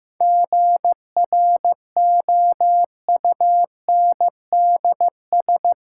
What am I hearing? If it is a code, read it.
Morse code, heard as a keyed tone, about 15 wpm, 700 Hz. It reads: GROUNDS
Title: Sentences from Top 1300 Words - New Word Drill - 15wpm